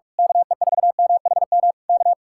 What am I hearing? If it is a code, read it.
Morse code, heard as a keyed tone, about 45 wpm, 700 Hz. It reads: KE4MHM K
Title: Speed-Racing Common QSO Phrases Encore - 45wpm